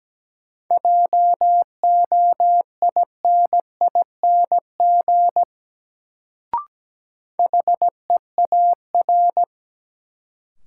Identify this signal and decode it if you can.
Morse code, heard as a keyed tone, about 17 wpm, 700 Hz. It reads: JOINING HEAR
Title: Sets of 5 Words - New Word Drill - 17wpm